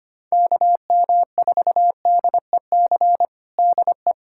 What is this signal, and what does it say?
Morse code, heard as a keyed tone, about 25 wpm, 700 Hz. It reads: KM4DEC DE
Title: USA Callsign DE Callsign 25wpm